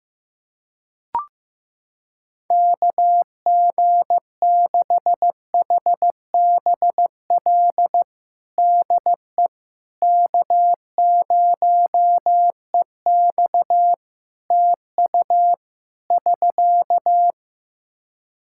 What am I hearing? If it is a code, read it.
Morse code, heard as a keyed tone, about 15 wpm, 700 Hz. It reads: KG6HBL DE K0EX TU <SK>
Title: K1USN SST QSO Practice 15wpm